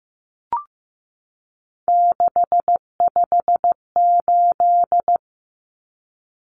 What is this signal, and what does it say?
Morse code, heard as a keyed tone, about 15 wpm, 700 Hz. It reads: 658
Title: Numbers - 3-Digit 15wpm